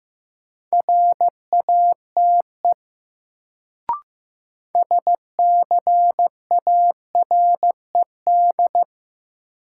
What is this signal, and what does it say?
Morse code, heard as a keyed tone, about 15 wpm, 700 Hz. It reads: RATE SCARED
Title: Sentences from Top 700 Words - New Word Drill - 15wpm